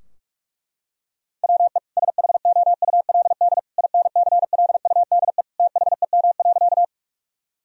Morse code, heard as a keyed tone, about 46 wpm, 700 Hz. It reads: WE SHOULD INCLUDE THEM.
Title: Sentences from Top 1000 Words - Review Part 3 - 45wpm